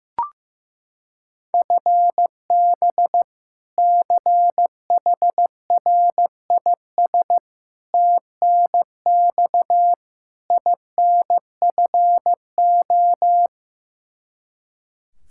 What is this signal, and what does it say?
Morse code, heard as a keyed tone, about 15 wpm, 700 Hz. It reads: FB CHRIS TNX INFO
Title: Practice QSO - Exchange 3 - More Info - 15wpm